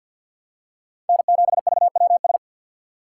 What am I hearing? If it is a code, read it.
Morse code, heard as a keyed tone, about 47 wpm, 700 Hz. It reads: N7VWS
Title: USA Call Signs - 5 Characters - 3x Character Spacing - 50wpm